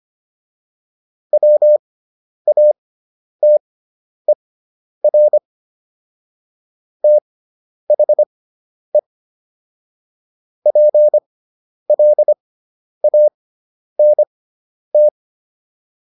Morse code, heard as a keyed tone, about 25 wpm, spaced as for 10 wpm, 600 Hz. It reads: WATER THE PLANT